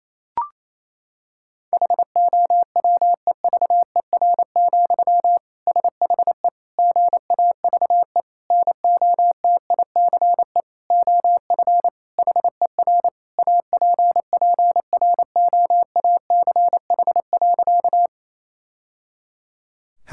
Morse code, heard as a keyed tone, about 28 wpm, 700 Hz. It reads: HOWEVER, SHE GAVE NOTICE OF HER APPROACH.
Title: Sentences from Top 1000 Words - Review Part 6 - 28wpm